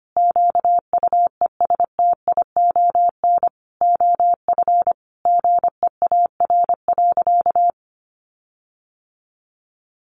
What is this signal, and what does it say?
Morse code, heard as a keyed tone, about 25 wpm, 700 Hz. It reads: QUESTION OF GEAR.
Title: Sentences from Top 1400 Words 25wpm